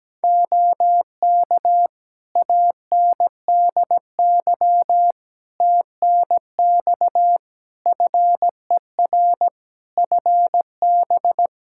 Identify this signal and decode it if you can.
Morse code, heard as a keyed tone, about 17 wpm, 700 Hz. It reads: OK ANDY TNX FER FB
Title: K1USN SST QSO Practice 17wpm